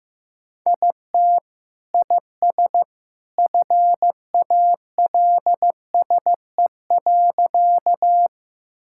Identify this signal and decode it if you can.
Morse code, heard as a keyed tone, about 16 wpm, 700 Hz. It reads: IT IS FALSE.
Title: Sentences from Top 1600 Words 15wpm